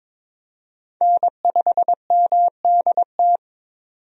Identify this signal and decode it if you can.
Morse code, heard as a keyed tone, about 22 wpm, 700 Hz. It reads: N5MDT
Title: USA Call Signs - 5 Characters - 22wpm